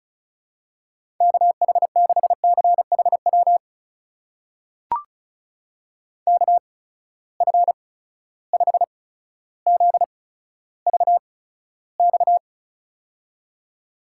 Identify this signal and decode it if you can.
Morse code, heard as a keyed tone, about 34 wpm, spaced as for 15 wpm, 700 Hz. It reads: KH6CHW KF5ZVX
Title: USA Call Signs - 6 Characters - 8x Character Spacing - 35wpm